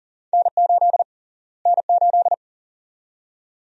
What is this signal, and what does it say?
Morse code, heard as a keyed tone, about 37 wpm, 700 Hz. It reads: N8 N8
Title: USA Call Sign Prefixes - Repeated 2x - 40wpm